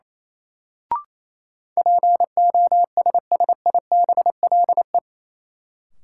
Morse code, heard as a keyed tone, about 28 wpm, 700 Hz. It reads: POSSIBLE
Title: Sentences from Top 500 Words - New Word Drill - 28wpm